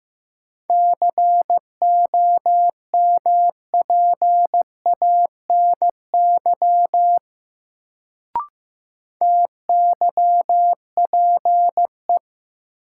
Morse code, heard as a keyed tone, about 15 wpm, 700 Hz. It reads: COMPANY TYPE
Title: Sentences from Top 600 Words - New Word Drill - 15wpm